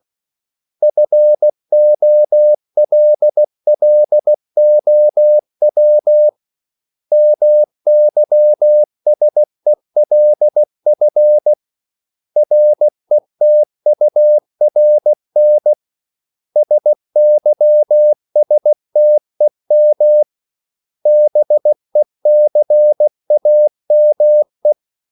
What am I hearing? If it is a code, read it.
Morse code, heard as a keyed tone, about 16 wpm, 600 Hz. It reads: FOLLOW MYSELF RETURN SYSTEM BECAME
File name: CW / Practice / 500MostCommonEnglishWords16